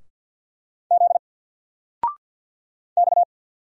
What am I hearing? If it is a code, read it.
Morse code, heard as a keyed tone, about 50 wpm, 700 Hz. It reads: Z X
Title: Single Letter-Number 50wpm